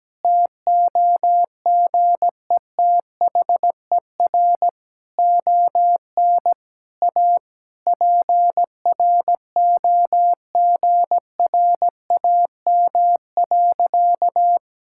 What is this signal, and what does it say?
Morse code, heard as a keyed tone, about 17 wpm, 700 Hz. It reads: TOGETHER ON A PROGRAM.
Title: Sentences from Top 1000 Words - Review Part 5 - 17wpm